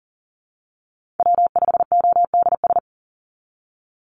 Morse code, heard as a keyed tone, about 39 wpm, 700 Hz. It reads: W5ODS
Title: NA SOTA Chaser Callsigns 40wpm